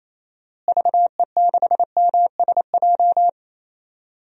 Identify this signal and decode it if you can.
Morse code, heard as a keyed tone, about 28 wpm, 700 Hz. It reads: VE6MSJ
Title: DXCC-100 Call Signs - 6 Characters - Repeated 3x - 28wpm